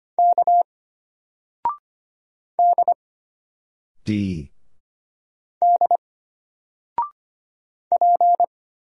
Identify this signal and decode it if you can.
Morse code, heard as a keyed tone, about 25 wpm, 700 Hz. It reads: K D D P
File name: Single-Letter-Number-25wpm